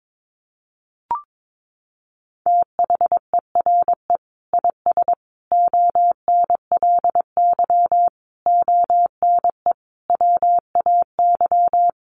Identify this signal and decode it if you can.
Morse code, heard as a keyed tone, about 22 wpm, 700 Hz. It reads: THERE IS ONLY ONE WAY
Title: Sentences from Top 1200 Words 22wpm